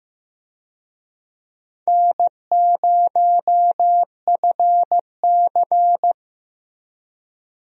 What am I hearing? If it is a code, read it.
Morse code, heard as a keyed tone, about 15 wpm, 700 Hz. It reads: N0FC
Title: USA Call Signs - 4 Characters - Repeated 2x - 15wpm